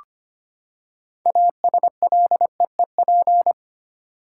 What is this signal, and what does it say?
Morse code, heard as a keyed tone, about 25 wpm, 700 Hz. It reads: ASLEEP
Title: Six Letter Words - 25wpm